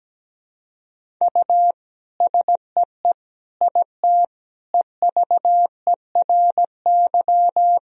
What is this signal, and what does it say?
Morse code, heard as a keyed tone, about 17 wpm, 700 Hz. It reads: U SEE IT EVERY
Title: Sentences from Top 200 Words and Abbreviations 17wpm